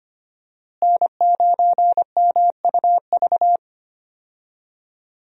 Morse code, heard as a keyed tone, about 25 wpm, 700 Hz. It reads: N9MUV
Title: USA Call Signs - 5 Characters - Repeated 2x - 25wpm